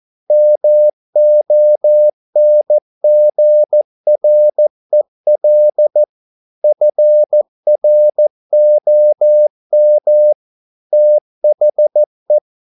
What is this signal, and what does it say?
Morse code, heard as a keyed tone, about 14 wpm, 600 Hz. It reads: MONGREL FROM THE
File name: mp3 files / a dogs mistake - 14wpm